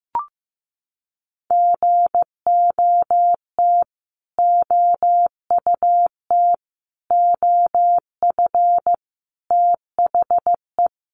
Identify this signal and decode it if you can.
Morse code, heard as a keyed tone, about 15 wpm, 700 Hz. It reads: GOT OUT OF THE
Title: Sets of 5 Words Encore 15wpm